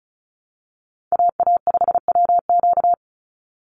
Morse code, heard as a keyed tone, about 34 wpm, 700 Hz. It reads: AA5WQ
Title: USA Call Signs 35wpm